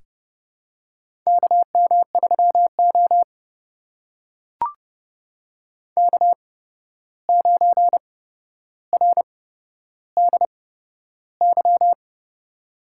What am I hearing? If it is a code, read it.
Morse code, heard as a keyed tone, about 28 wpm, spaced as for 12 wpm, 700 Hz. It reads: KM3O K9RDY
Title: USA Call Signs - 8x Character Spacing - 30wpm